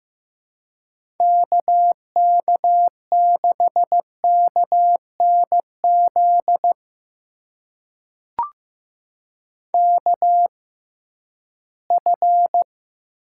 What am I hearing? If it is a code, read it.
Morse code, heard as a keyed tone, about 15 wpm, spaced as for 11 wpm, 700 Hz. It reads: KK6KNZ KF
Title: USA Call Signs - 6 Characters - 6x Character Spacing - 15wpm